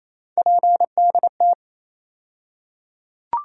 Morse code, heard as a keyed tone, about 28 wpm, 700 Hz. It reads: PDT
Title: USA Call Sign Postfixes - 3 Letters 28wpm